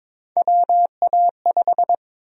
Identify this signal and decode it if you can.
Morse code, heard as a keyed tone, about 22 wpm, 700 Hz. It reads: WA5
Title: USA Call Sign Prefixes - 22wpm